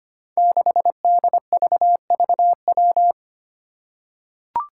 Morse code, heard as a keyed tone, about 25 wpm, 700 Hz. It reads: 6DVVW
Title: Five Challenging Letter-Number Combinations 25wpm